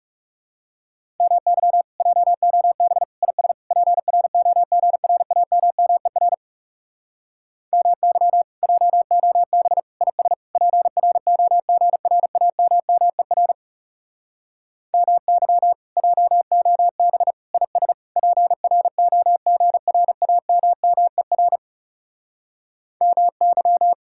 Morse code, heard as a keyed tone, about 40 wpm, 700 Hz. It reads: MY JOB IS PROGRAMMER MY JOB IS PROGRAMMER MY JOB IS PROGRAMMER MY